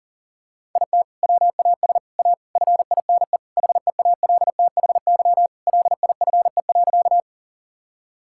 Morse code, heard as a keyed tone, about 40 wpm, 700 Hz. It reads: IT WAS A FINE HEALTHY LIFE.
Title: Sentences from Top 1000 Words - Review Part 5 - 40wpm